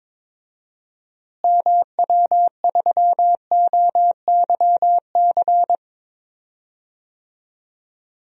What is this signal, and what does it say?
Morse code, heard as a keyed tone, about 22 wpm, 700 Hz. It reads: MW3OYC
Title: DXCC-100 Call Signs - 6 Characters - Repeated 2x - 22wpm